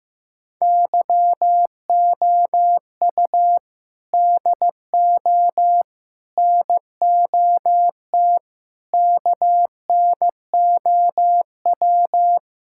Morse code, heard as a keyed tone, about 15 wpm, 700 Hz. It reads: YOU DO NOT KNOW
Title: Sentences from Top 1000 Words - Review Part 6 - 15wpm